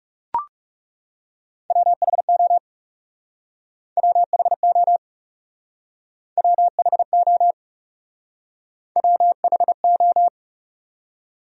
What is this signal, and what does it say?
Morse code, heard as a keyed tone, about 40 wpm, 700 Hz. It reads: WHO WHO WHO WHO